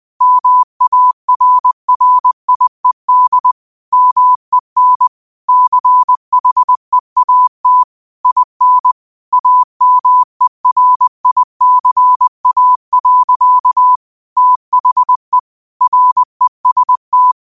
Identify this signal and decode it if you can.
Morse code, heard as a keyed tone, about 20 wpm, 1000 Hz. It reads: MARRIED MEN CHEAT IN AMERICA. THE REST
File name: qotdfiles / 20wpm